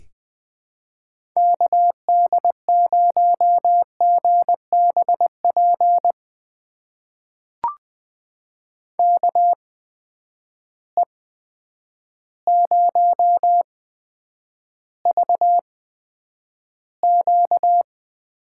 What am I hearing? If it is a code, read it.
Morse code, heard as a keyed tone, about 20 wpm, spaced as for 9 wpm, 700 Hz. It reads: KD0GBP KE0VQ
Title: USA Call Signs - 6 Characters - 8x Character Spacing - 20wpm